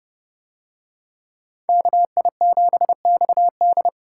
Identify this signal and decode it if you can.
Morse code, heard as a keyed tone, about 28 wpm, 700 Hz. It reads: KI7XD